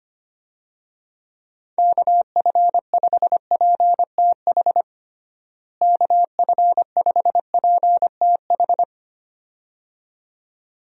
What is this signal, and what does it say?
Morse code, heard as a keyed tone, about 25 wpm, 700 Hz. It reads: KF5PTH KF5PTH
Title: USA Call Signs - 6 Characters - Repeated 2x - 25wpm